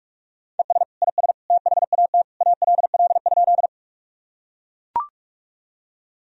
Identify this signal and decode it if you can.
Morse code, heard as a keyed tone, about 43 wpm, 700 Hz. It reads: ES IS THAT ALL?
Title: Sentences from Top 100 Words and Abbreviations 45wpm